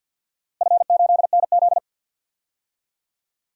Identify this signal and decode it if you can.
Morse code, heard as a keyed tone, about 49 wpm, 700 Hz. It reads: F8NZ
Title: DXCC-100 Call Signs - 4 Characters - Repeated 2x - 50wpm